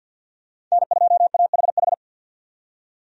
Morse code, heard as a keyed tone, about 50 wpm, 700 Hz. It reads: N2AHH
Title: USA Call Signs - 5 Characters - 50wpm